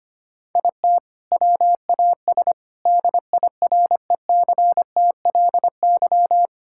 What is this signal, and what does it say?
Morse code, heard as a keyed tone, about 25 wpm, 700 Hz. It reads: IT WAS DIRECTLY
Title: Sentences from Top 1000 Words - Review Part 5 - 25wpm